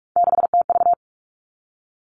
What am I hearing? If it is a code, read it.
Morse code, heard as a keyed tone, about 46 wpm, 700 Hz. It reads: 6TV
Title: Three Challenging Letter-Number Combinations 45wpm